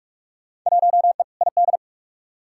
Morse code, heard as a keyed tone, about 46 wpm, 700 Hz. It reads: 1E ID
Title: Field Day Exchanges - 45wpm